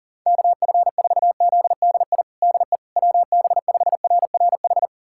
Morse code, heard as a keyed tone, about 37 wpm, 700 Hz. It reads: KF4ZDI DE WB5RRH
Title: USA Callsign DE Callsign - 6 Characters - 40wpm